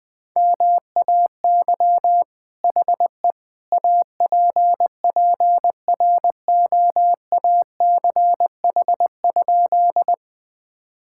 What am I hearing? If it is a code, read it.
Morse code, heard as a keyed tone, about 20 wpm, 700 Hz. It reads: MAY HE APPROACH?
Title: Sentences from Top 1000 Words - Review Part 6 - 20wpm